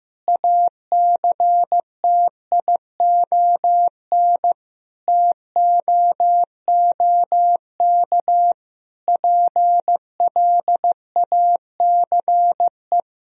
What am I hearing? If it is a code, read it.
Morse code, heard as a keyed tone, about 15 wpm, 700 Hz. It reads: ACTION TOOK PLACE
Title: Sentences from Top 1000 Words - Review Part 4 - 15wpm